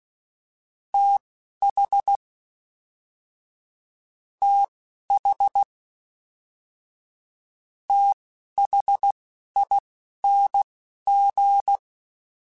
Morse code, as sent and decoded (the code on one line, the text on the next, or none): - .... / - .... / - .... .. -. --.
TH TH THING